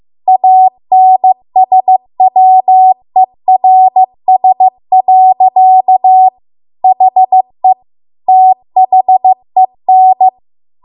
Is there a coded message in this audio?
Yes: Morse code from 0.3 to 10.3 s, heard as a keyed tone, about 15 wpm, 750 Hz. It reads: ANSWERS. HE THEN